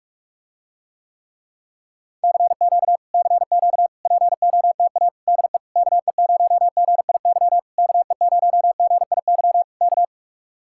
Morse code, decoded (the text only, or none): CQ CQ POTA DE KE0GIY KE0GIY K